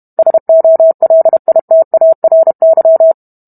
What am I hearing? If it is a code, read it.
Morse code, heard as a keyed tone, about 32 wpm, 650 Hz. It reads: SOLITARY